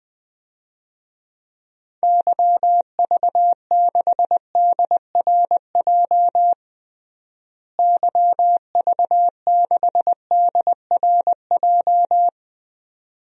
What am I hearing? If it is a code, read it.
Morse code, heard as a keyed tone, about 20 wpm, 700 Hz. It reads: YV6DRJ YV6DRJ
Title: DXCC-100 Call Signs - 6 Characters - Repeated 2x - 20wpm